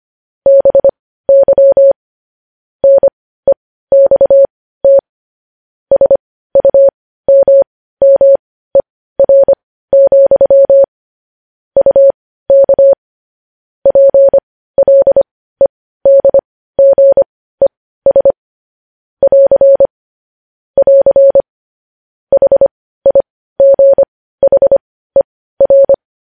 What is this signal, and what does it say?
Morse code, heard as a keyed tone, about 25 wpm, spaced as for 15 wpm, 550 Hz. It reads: BY NEXT SUMMER, UK PLEDGES <AR> <AR> HIGHER
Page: News Headlines in Morse Code at 15 WPM